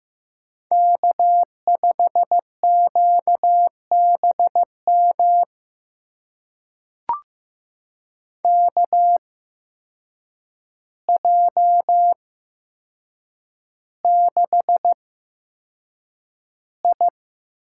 Morse code, heard as a keyed tone, about 15 wpm, spaced as for 7 wpm, 700 Hz. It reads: K5QBM KJ6I
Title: USA Call Signs - 5 Characters - 8x Character Spacing - 15wpm